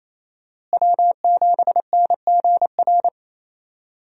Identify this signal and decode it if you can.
Morse code, heard as a keyed tone, about 28 wpm, 700 Hz. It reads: W7NGR
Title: USA Call Signs - 5 Characters - Repeated 2x - 28wpm